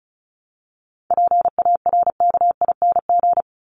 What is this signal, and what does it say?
Morse code, heard as a keyed tone, about 34 wpm, 700 Hz. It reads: PARKING
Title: Top 1501-1750 Words 35wpm